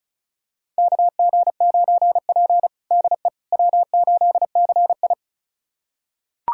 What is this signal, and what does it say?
Morse code, heard as a keyed tone, about 34 wpm, 700 Hz. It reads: KG9P DE W8CI
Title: USA Callsign DE Callsign - 4 Characters - 35wpm